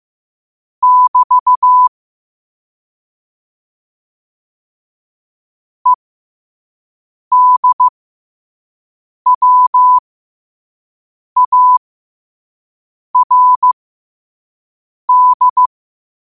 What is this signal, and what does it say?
Morse code, heard as a keyed tone, about 15 wpm, spaced as for 5 wpm, 1000 Hz. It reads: <BT> EDWARD